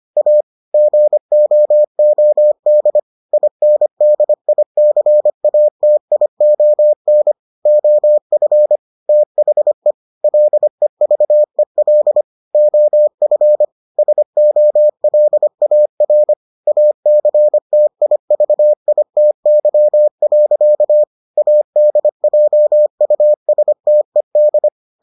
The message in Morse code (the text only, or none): A GOOD INDICATION OF THE LEVEL OF SOLAR ACTIVITY. ADJUSTED